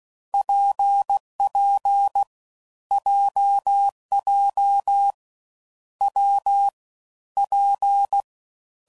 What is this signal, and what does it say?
Morse code, heard as a keyed tone, about 16 wpm, 800 Hz. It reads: PP JJ W P
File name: Combinations / Difficult Combos / WPJ WPJ1